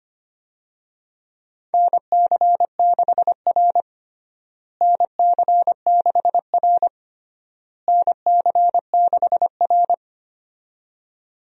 Morse code, heard as a keyed tone, about 25 wpm, 700 Hz. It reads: NC6R NC6R NC6R
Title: USA Call Signs - 4 Characters - Repeated 3x - 25wpm